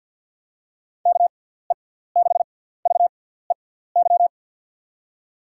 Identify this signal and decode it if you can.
Morse code, heard as a keyed tone, about 48 wpm, spaced as for 17 wpm, 700 Hz. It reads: KE6VEY
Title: USA Call Signs - 6 Characters - 6x Character Spacing - 50wpm